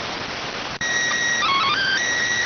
SSTV (slow-scan television), PD 290.